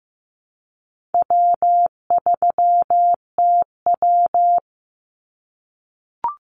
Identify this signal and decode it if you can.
Morse code, heard as a keyed tone, about 15 wpm, 700 Hz. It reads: W3TW
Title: Easy Call Signs - 15wpm